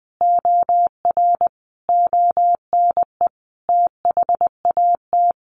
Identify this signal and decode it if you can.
Morse code, heard as a keyed tone, about 20 wpm, 700 Hz. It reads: OR ONE THAT
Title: A Sound Of Thunder 20wpm